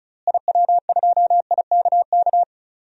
Morse code, heard as a keyed tone, about 36 wpm, 700 Hz. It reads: IW2IKK
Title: DXCC-100 Call Signs - 6 Characters - Repeated 2x - 35wpm